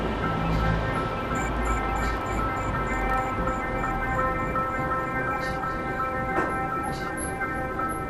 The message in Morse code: none

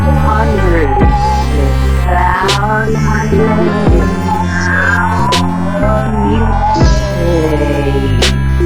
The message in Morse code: none

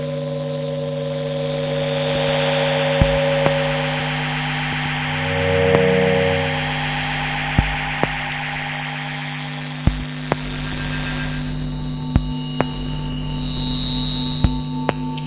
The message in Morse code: none

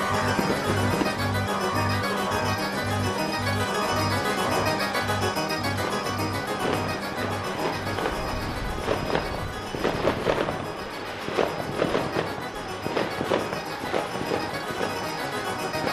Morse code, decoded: none

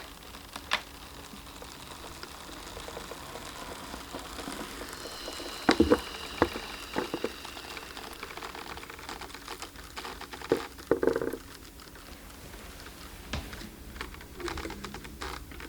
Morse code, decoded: none

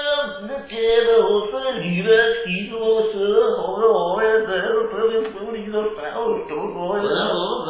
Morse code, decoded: none

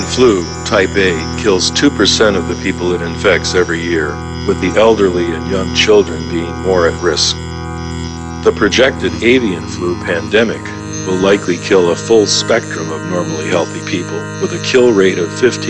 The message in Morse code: none

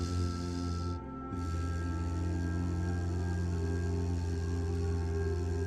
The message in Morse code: none

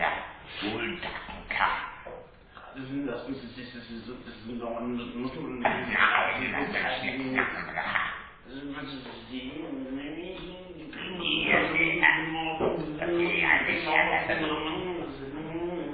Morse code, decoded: none

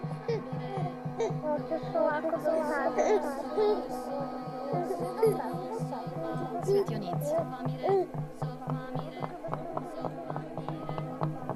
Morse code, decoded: none

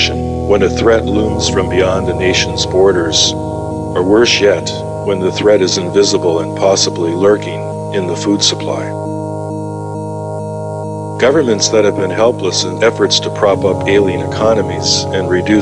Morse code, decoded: none

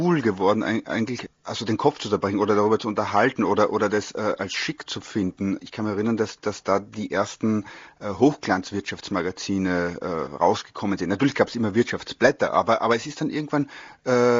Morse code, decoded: none

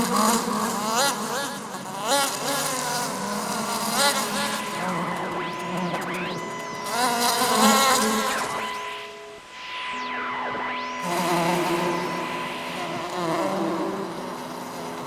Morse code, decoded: none

